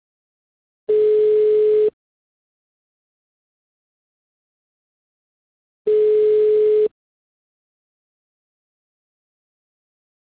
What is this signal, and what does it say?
Ringback tone (Europe, 425 Hz).